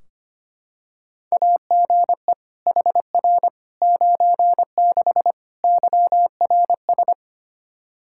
Morse code, heard as a keyed tone, about 25 wpm, 700 Hz. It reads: AGE HR 96 YRS
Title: Practice QSO - Exchange 3 - More Info - 25wpm